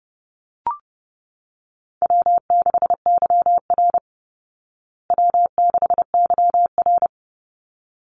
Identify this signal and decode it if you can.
Morse code, heard as a keyed tone, about 30 wpm, 700 Hz. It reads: W6YR W6YR
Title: Call Signs Repeated 3x - 30wpm